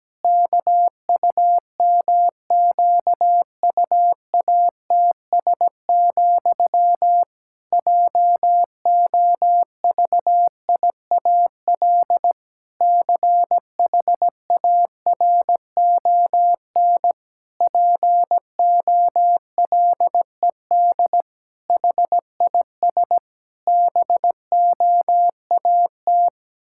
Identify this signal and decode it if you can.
Morse code, heard as a keyed tone, about 17 wpm, 700 Hz. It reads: KUMQUATS, JOVIAL CHARON POLED HIS BOAT
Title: Pangram Challenge v2 - 17wpm